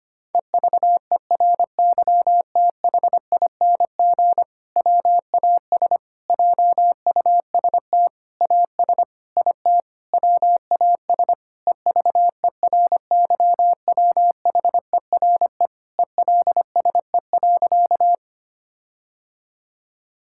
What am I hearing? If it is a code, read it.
Morse code, heard as a keyed tone, about 25 wpm, 700 Hz. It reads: EVERYTHING WAS JUST AS IT WAS EVERYWHERE ELSE.
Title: Sentences from Top 1600 Words 25wpm